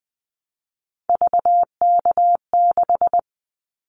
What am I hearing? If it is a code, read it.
Morse code, heard as a keyed tone, about 20 wpm, 700 Hz. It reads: VK6
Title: Callsign Prefixes for Top 50 DXCC - 20wpm